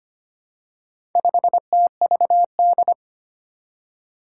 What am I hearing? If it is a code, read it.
Morse code, heard as a keyed tone, about 25 wpm, 700 Hz. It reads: 5TVD
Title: Four Challenging Letter-Number Combinations 25wpm